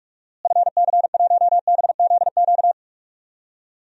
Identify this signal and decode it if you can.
Morse code, heard as a keyed tone, about 46 wpm, 700 Hz. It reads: FC1BZQ
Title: DXCC-100 Call Signs - 6 Characters - Repeated 2x - 45wpm